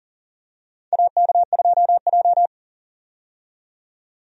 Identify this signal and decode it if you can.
Morse code, heard as a keyed tone, about 39 wpm, 700 Hz. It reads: AK2J
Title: USA Call Signs - 4 Characters - 40wpm